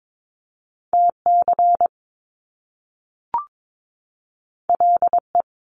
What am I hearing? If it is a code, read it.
Morse code, heard as a keyed tone, about 22 wpm, 700 Hz. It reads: TC LE
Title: Top Two Letter Combinations in Words 22wpm